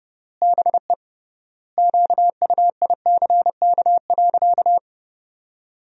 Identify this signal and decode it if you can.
Morse code, heard as a keyed tone, about 30 wpm, 700 Hz. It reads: BE QUICK.
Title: 3x Word-Spacing - Sentences from Top 500 Words - 30wpm